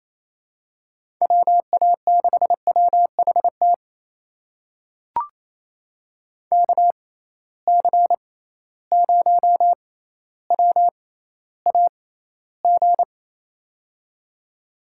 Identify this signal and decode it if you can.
Morse code, heard as a keyed tone, about 28 wpm, spaced as for 14 wpm, 700 Hz. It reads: WA6WHT KC0WAG
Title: USA Call Signs - 6 Characters - 6x Character Spacing - 28wpm